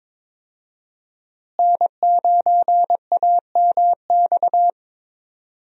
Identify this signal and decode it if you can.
Morse code, heard as a keyed tone, about 22 wpm, 700 Hz. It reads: N9AMX
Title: USA Call Signs - 5 Characters - Repeated 3x - 22wpm